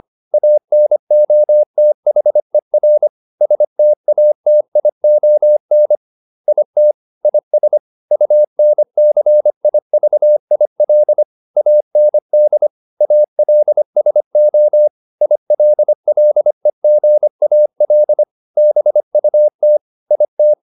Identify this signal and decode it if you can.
Morse code, heard as a keyed tone, about 25 wpm, 600 Hz. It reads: ANOTHER STATION IT IS UNCIVIL AND ALSO ILLEGAL BUT IT